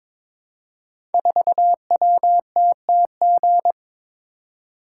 Morse code, heard as a keyed tone, about 22 wpm, 700 Hz. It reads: 4WTTG
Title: Code Groups - 5 Letter:Numbers - 22wpm